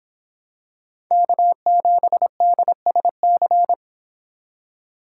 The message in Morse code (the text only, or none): K7DSC